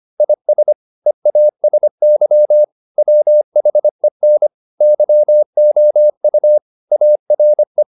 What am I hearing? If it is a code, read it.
Morse code, heard as a keyed tone, about 25 wpm, 600 Hz. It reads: IS EASY WHEN YOU ARE